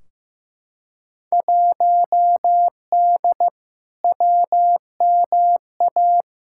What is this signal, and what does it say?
Morse code, heard as a keyed tone, about 15 wpm, 700 Hz. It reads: 1D WMA
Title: Field Day Exchanges - 15wpm